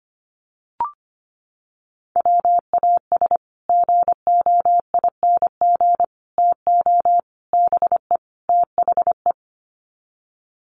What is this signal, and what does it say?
Morse code, heard as a keyed tone, about 25 wpm, 700 Hz. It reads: WAS GOING TO BE THE
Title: Sets of 5 Words Encore2 25wpm